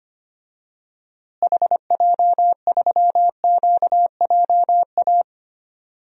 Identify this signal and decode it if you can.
Morse code, heard as a keyed tone, about 25 wpm, 700 Hz. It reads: HJ3QJA
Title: DXCC-100 Call Signs - 6 Characters - Repeated 2x - 25wpm